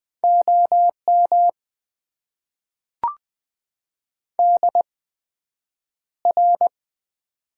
Morse code, heard as a keyed tone, about 20 wpm, spaced as for 9 wpm, 700 Hz. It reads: OM DR
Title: Top 90 QSO Elements - 8x Character Spacing - 20wpm